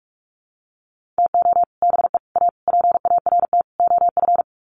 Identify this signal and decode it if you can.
Morse code, heard as a keyed tone, about 44 wpm, 700 Hz. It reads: TO BE A PART OF
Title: Sets of 5 Words 45wpm